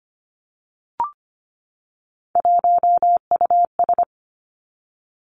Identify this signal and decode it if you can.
Morse code, heard as a keyed tone, about 25 wpm, 700 Hz. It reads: 1US
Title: Three Challenging Letter-Number Combinations 25wpm